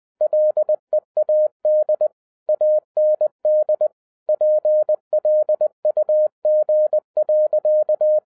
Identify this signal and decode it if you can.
Morse code, heard as a keyed tone, about 20 wpm, 610 Hz. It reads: LEAD AND PLUG.
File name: 2011-07-mp3 / n43 Czech Army Morse Keys 20wpm